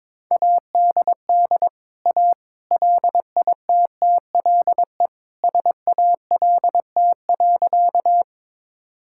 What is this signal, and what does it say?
Morse code, heard as a keyed tone, about 21 wpm, 700 Hz. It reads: ADD A LITTLE SALT.
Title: Sentences from Top 1000 Words - Review Part 5 - 22wpm